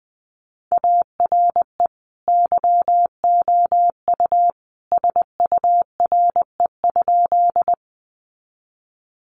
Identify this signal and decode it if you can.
Morse code, heard as a keyed tone, about 20 wpm, 700 Hz. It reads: ARE YOU SURE?
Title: Sentences from Top 300 Words 20wpm